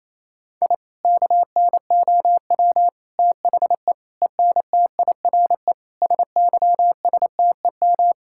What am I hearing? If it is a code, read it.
Morse code, heard as a keyed tone, about 28 wpm, 700 Hz. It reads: I KNOW THE ENTIRE SYSTEM
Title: Sentences from Top 1000 Words - Review Part 5 - 28wpm